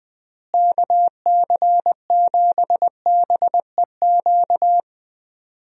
Morse code, heard as a keyed tone, about 20 wpm, 700 Hz. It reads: KC7BEQ
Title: USA Call Signs - 6 Characters - Repeated 2x - 20wpm